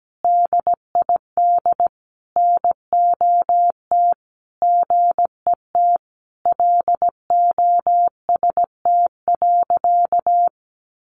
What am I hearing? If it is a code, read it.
Morse code, heard as a keyed tone, about 17 wpm, 700 Hz. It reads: DID NOT GET LOST.
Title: Sentences from Top 1400 Words 17wpm